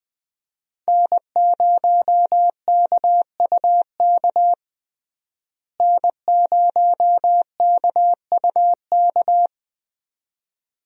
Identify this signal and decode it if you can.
Morse code, heard as a keyed tone, about 20 wpm, 700 Hz. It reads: N0KUK N0KUK
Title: USA Call Signs - 5 Characters - Repeated 2x - 20wpm